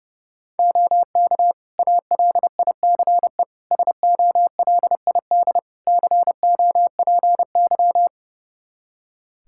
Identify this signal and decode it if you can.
Morse code, heard as a keyed tone, about 31 wpm, 700 Hz. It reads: OK ALICE SOLID COPY
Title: Practice QSO - Exchange 3 - More Info - 30wpm